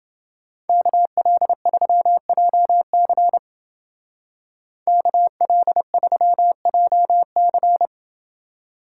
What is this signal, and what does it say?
Morse code, heard as a keyed tone, about 28 wpm, 700 Hz. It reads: KL3JC KL3JC